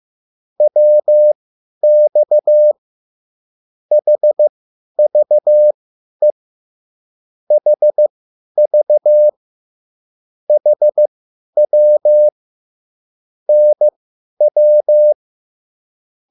Morse code, heard as a keyed tone, about 15 wpm, spaced as for 10 wpm, 600 Hz. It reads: WX HVE HV HW NW